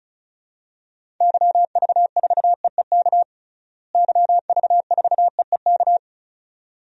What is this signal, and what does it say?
Morse code, heard as a keyed tone, about 36 wpm, 700 Hz. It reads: YV4EEK YV4EEK
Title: DXCC-100 Call Signs - 6 Characters - Repeated 2x - 35wpm